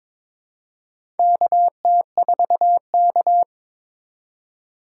Morse code, heard as a keyed tone, about 22 wpm, 700 Hz. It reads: KT4K
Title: USA Call Signs - 4 Characters - 2x Character Spacing - 22wpm